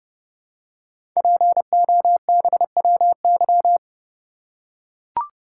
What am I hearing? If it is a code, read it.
Morse code, heard as a keyed tone, about 30 wpm, 700 Hz. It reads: POBWY
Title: Code Groups - 5 Letters - 30wpm